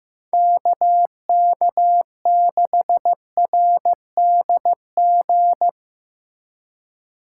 Morse code, heard as a keyed tone, about 17 wpm, 700 Hz. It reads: KK6RDG
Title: Speed Racing - USA 2x3 Call Signs 15wpm